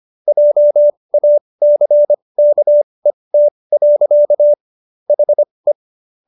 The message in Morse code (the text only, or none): JACKET. HE